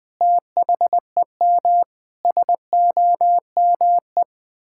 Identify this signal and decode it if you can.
Morse code, heard as a keyed tone, about 20 wpm, 700 Hz. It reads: THEM SOME
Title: Sentences from Top 1000 Words - Review Part 6 - 20wpm